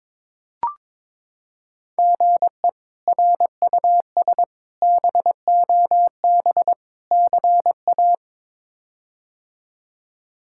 Morse code, heard as a keyed tone, about 22 wpm, 700 Hz. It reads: GE RUS BOB CA
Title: K1USN SST QSO Practice - 22wpm